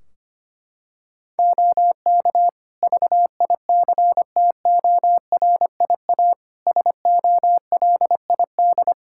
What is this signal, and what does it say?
Morse code, heard as a keyed tone, about 25 wpm, 700 Hz. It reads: OK VICTORIA SOLID
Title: Practice QSO - Exchange 3 - More Info - 25wpm